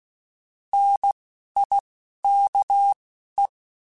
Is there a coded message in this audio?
Yes: Morse code from 0.7 to 3.5 s, heard as a keyed tone, about 16 wpm, 800 Hz. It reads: N I K E